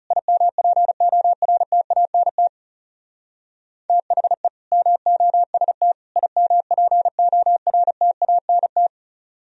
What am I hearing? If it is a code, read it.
Morse code, heard as a keyed tone, about 35 wpm, 700 Hz. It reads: IMPORTANT THE MOST IMPORTANT
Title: Speed-Racing - Sets of 3 Words 30wpm